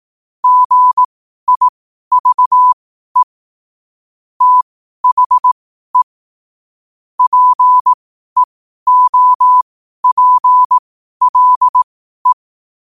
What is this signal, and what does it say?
Morse code, heard as a keyed tone, about 18 wpm, spaced as for 12 wpm, 1000 Hz. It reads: GIVE THE PEOPLE
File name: qotdfiles / 12wpm